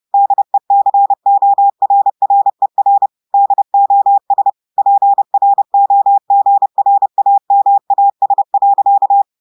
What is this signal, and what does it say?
Morse code, heard as a keyed tone, about 30 wpm, 800 Hz. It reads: DECORRER DOS PROGRAMAS.
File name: portugais texte en clair 2 vitesse 30 mots minutes